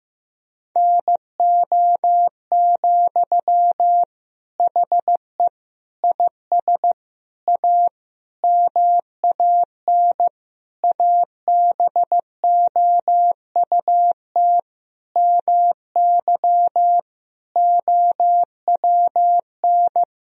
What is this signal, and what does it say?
Morse code, heard as a keyed tone, about 15 wpm, 700 Hz. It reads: NO, HE IS A MAN ABOUT MY OWN
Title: Sentences from Top 1000 Words - Review Part 2 - 15wpm